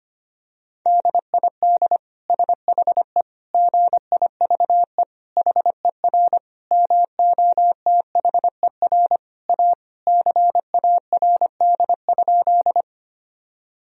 Morse code, heard as a keyed tone, about 25 wpm, 700 Hz. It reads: DID SHE GIVE HER MOTHER A CARD?
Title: Sentences from Top 1000 Words - Review Part 4 - 25wpm